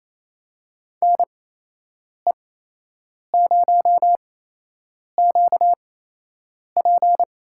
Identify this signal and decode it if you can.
Morse code, heard as a keyed tone, about 28 wpm, spaced as for 8 wpm, 700 Hz. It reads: NE0QP
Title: USA Call Signs - 5 Characters - 8x Character Spacing - 28wpm